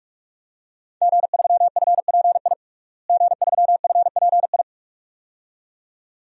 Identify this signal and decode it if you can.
Morse code, heard as a keyed tone, about 46 wpm, 700 Hz. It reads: G3FPI G3FPI
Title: DXCC-100 Call Signs - 5 Characters - Repeated 2x - 45wpm